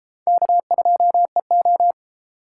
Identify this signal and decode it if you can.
Morse code, heard as a keyed tone, about 32 wpm, 700 Hz. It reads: K2EO